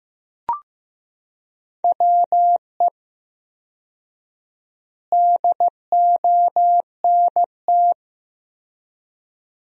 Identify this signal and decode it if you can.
Morse code, heard as a keyed tone, about 15 wpm, 700 Hz. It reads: WE DONT
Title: 4x Word-Spacing - Sentences from Top 300 Words - 15wpm